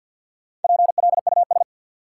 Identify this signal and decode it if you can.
Morse code, heard as a keyed tone, about 51 wpm, 700 Hz. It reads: PLUS
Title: Four Letter Words - 50wpm